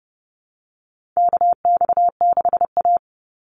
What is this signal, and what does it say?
Morse code, heard as a keyed tone, about 30 wpm, 700 Hz. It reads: KX6A
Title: NA SOTA Chaser Callsigns 30wpm